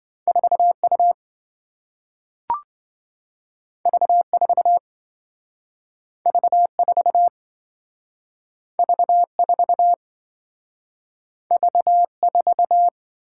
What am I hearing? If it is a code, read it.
Morse code, heard as a keyed tone, about 27 wpm, 700 Hz. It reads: 4U V4 V4 V4 V4